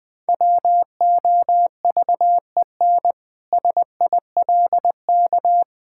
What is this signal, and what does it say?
Morse code, heard as a keyed tone, about 19 wpm, 700 Hz. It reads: WOVEN SILK
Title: Pangram Challenge v2 - 20wpm